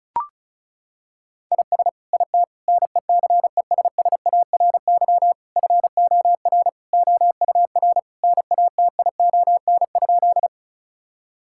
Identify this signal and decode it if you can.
Morse code, heard as a keyed tone, about 36 wpm, 700 Hz. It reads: IS IT NECESSARY FOR OUR NATION?
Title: Sentences from Top 1000 Words - Review Part 6 - 35wpm